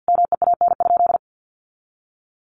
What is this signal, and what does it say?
Morse code, heard as a keyed tone, about 50 wpm, 700 Hz. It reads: MEAN?
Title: Sentences from Top 1300 Words 50wpm